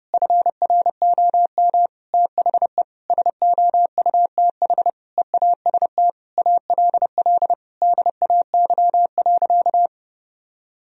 Morse code, heard as a keyed tone, about 29 wpm, 700 Hz. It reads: FROM THE SOUTH EAST ALL DAY.
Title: Sentences from Top 1000 Words - Review Part 3 - 30wpm